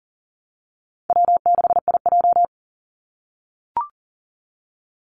Morse code, heard as a keyed tone, about 40 wpm, 700 Hz. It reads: W6IJ
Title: Call Signs - 40wpm